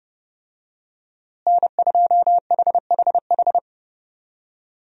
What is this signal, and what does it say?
Morse code, heard as a keyed tone, about 31 wpm, 700 Hz. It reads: N2HHH
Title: USA Call Signs - 5 Characters - Repeated 2x - 30wpm